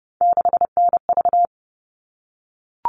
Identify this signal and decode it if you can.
Morse code, heard as a keyed tone, about 30 wpm, 700 Hz. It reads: 6NV
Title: All Cut Numbers - 3-Digit 30wpm